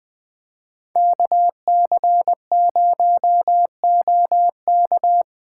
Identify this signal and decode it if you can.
Morse code, heard as a keyed tone, about 20 wpm, 700 Hz. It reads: KC0OK